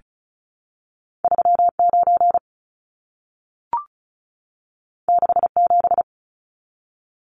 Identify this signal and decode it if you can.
Morse code, heard as a keyed tone, about 36 wpm, 700 Hz. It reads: 39 67
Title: Numbers - 2-Digit 35wpm